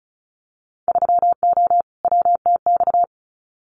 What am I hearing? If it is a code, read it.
Morse code, heard as a keyed tone, about 36 wpm, 700 Hz. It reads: 3O WTX
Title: Winter Field Day Exchanges Encore 35wpm